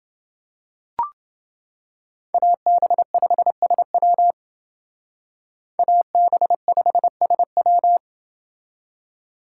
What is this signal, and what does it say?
Morse code, heard as a keyed tone, about 28 wpm, 700 Hz. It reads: AB5SW AB5SW